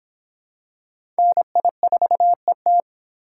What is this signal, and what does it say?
Morse code, heard as a keyed tone, about 25 wpm, 700 Hz. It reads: NI4ET